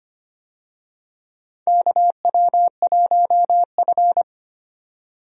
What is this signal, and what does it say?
Morse code, heard as a keyed tone, about 25 wpm, 700 Hz. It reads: KW1F
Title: USA Call Signs - 4 Characters - 25wpm